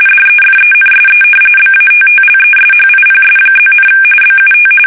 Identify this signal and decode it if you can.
RTTY, about 45.45 baud. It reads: DEFENSE BATTLESHIPS+, BUT
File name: fsk850 / battleships